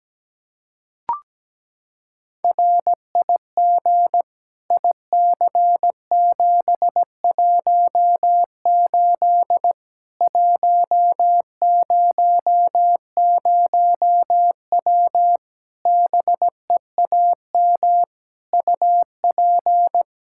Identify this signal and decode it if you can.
Morse code, heard as a keyed tone, about 17 wpm, 700 Hz. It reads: RIG IC718 100W BEAM UP
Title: K1USN SST QSO Practice 17wpm